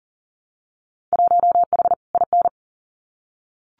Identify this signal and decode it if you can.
Morse code, heard as a keyed tone, about 37 wpm, 700 Hz. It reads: 1H IN
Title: Winter Field Day Exchanges 40wpm